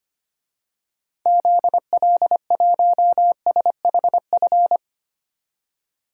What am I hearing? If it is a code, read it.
Morse code, heard as a keyed tone, about 25 wpm, 700 Hz. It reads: ZL1SHF
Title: DXCC-100 Call Signs - 6 Characters - 25wpm